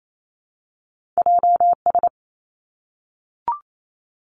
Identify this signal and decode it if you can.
Morse code, heard as a keyed tone, about 28 wpm, 700 Hz. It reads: JS
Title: Two Challenging Letter-Number Combinations 28wpm